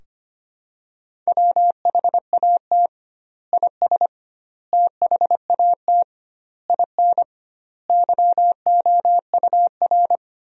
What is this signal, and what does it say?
Morse code, heard as a keyed tone, about 25 wpm, 700 Hz. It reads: WHAT IS THAT IN YOUR
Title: 2x Word-Spacing - Sentences from Top 200 Words - 25wpm